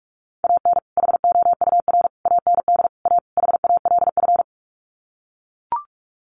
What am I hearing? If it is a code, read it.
Morse code, heard as a keyed tone, about 44 wpm, 700 Hz. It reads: AN HOUR AND A HALF
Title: Sets of 5 Words 45wpm